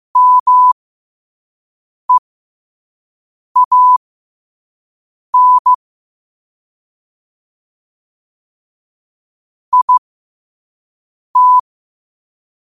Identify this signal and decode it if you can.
Morse code, heard as a keyed tone, about 15 wpm, spaced as for 5 wpm, 1000 Hz. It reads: MEAN IT
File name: qotdfiles / 5wpm